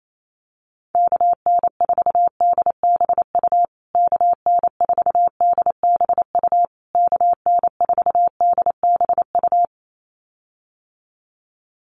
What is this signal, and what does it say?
Morse code, heard as a keyed tone, about 28 wpm, 700 Hz. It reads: KN4DBU KN4DBU KN4DBU
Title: USA Call Signs Repeated 3x 28wpm